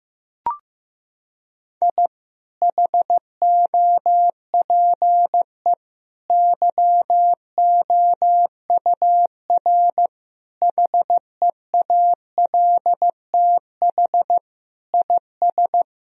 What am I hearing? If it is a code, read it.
Morse code, heard as a keyed tone, about 15 wpm, 700 Hz. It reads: I HOPE YOUR HEALTH IS
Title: Sentences from Top 1000 Words - Review Part 3 - 15wpm